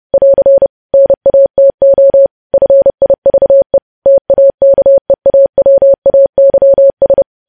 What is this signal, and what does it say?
Morse code, heard as a keyed tone, about 30 wpm, 550 Hz. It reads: <AR> NATO FIVE TAKEAWAYS
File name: podcast / News headlines-1x-30x30